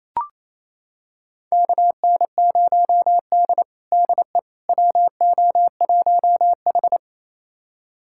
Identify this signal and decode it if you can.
Morse code, heard as a keyed tone, about 28 wpm, 700 Hz. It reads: KN0D DE WO1H
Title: USA Callsign DE Callsign - 4 Characters - 28wpm